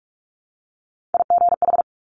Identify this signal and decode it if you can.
Morse code, heard as a keyed tone, about 46 wpm, 700 Hz. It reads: IGH
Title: Top Three Letter Combinations in Words 45wpm